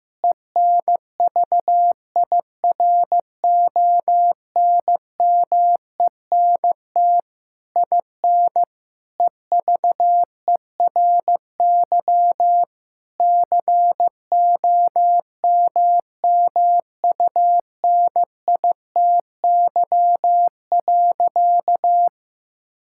Morse code, heard as a keyed tone, about 15 wpm, 700 Hz. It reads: ENVIRONMENT IN EVERY COMMUNITY.
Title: Sentences from Top 1000 Words - Review Part 5 - 15wpm